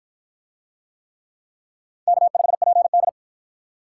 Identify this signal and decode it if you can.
Morse code, heard as a keyed tone, about 51 wpm, 700 Hz. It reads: K5PD